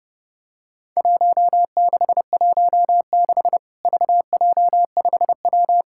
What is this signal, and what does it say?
Morse code, heard as a keyed tone, about 30 wpm, 700 Hz. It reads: 1616 VJ5W
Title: Worked All Europe DX Contest - QTC Exchange - 30wpm